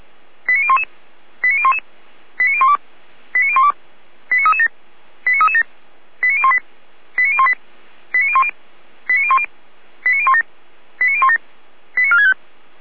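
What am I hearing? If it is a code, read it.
Five-tone selective call (ZVEI), heard as quick runs of five tones, tones of about 70 ms. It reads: 89011 89011 89012 89012 89337 89337 89018 89018 89010 89010 89017 89017 89465
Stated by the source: ZVEI five-tone selective call received with an SDR, archived by Feuerwehr Orlamünde.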